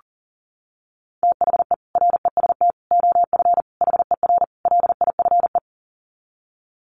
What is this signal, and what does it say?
Morse code, heard as a keyed tone, about 40 wpm, 700 Hz. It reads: THE REST OF HER LIFE
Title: Sets of 5 Words Encore 40wpm